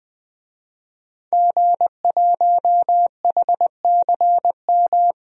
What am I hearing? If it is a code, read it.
Morse code, heard as a keyed tone, about 20 wpm, 700 Hz. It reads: G1HCM
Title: DXCC-100 Call Signs - 5 Characters - 20wpm